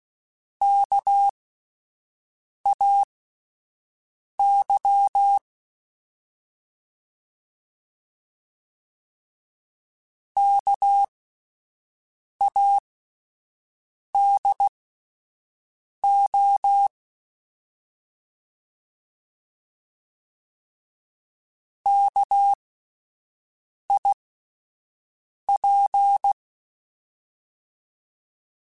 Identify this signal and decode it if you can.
Morse code, heard as a keyed tone, about 16 wpm, spaced as for 5 wpm, 800 Hz. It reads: KAY KADO KIP